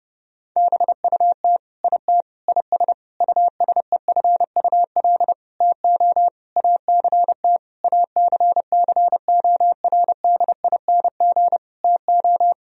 Morse code, heard as a keyed tone, about 30 wpm, 700 Hz. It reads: BUT IT IS USEFUL TO ACT ACCORDING TO
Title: Sentences from Top 1000 Words - Review Part 4 - 30wpm